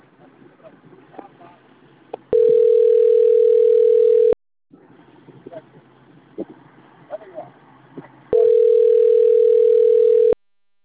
Ringback tone (North America), 440 + 480 Hz.